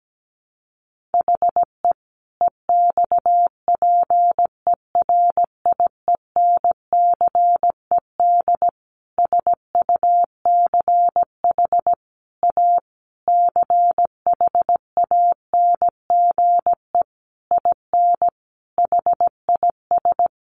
Morse code, heard as a keyed tone, about 17 wpm, 700 Hz. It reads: HE EXPERIENCED SUCH A CHANGE IN HIS
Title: Sentences from Top 1100 Words 17wpm